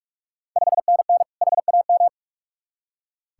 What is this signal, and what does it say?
Morse code, heard as a keyed tone, about 44 wpm, 700 Hz. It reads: 5NN HAM
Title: Ontario QSO Party Exchange 45wpm